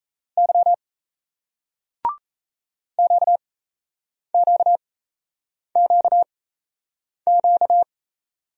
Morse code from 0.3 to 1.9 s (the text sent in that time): Y